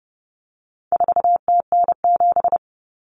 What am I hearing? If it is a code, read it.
Morse code, heard as a keyed tone, about 30 wpm, 700 Hz. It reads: 4TN7
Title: Common Cut Numbers - 4-Digit 30wpm